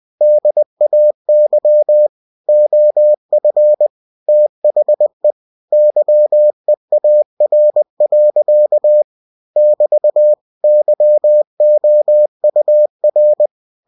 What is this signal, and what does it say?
Morse code, heard as a keyed tone, about 20 wpm, 600 Hz. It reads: DAY OF THE YEAR. <BT> YOUR
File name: cw-books / winnie-the-pooh / winnie-20wpm-page-0046